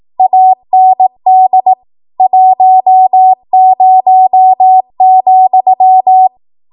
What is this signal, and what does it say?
Morse code, heard as a keyed tone, about 18 wpm, 750 Hz. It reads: AND 10,
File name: Archive / 18 WPM / 18WPM